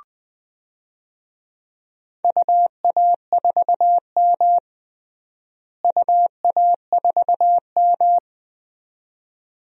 Morse code, heard as a keyed tone, about 20 wpm, 700 Hz. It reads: UA4M UA4M
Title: DXCC-100 Call Signs - 4 Characters - Repeated 2x - 20wpm